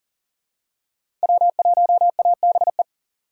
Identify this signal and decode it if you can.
Morse code, heard as a keyed tone, about 37 wpm, 690 Hz. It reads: W1ABE
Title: USA Call Signs - 5 Characters - 3x Character Spacing - 40wpm